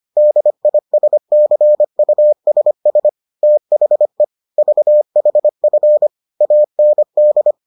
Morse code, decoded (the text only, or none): DISCUSS THE VHF AND